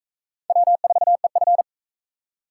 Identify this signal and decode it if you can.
Morse code, heard as a keyed tone, about 38 wpm, 700 Hz. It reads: W4EF